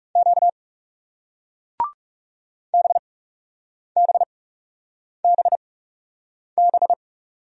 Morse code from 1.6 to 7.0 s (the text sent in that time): BBBB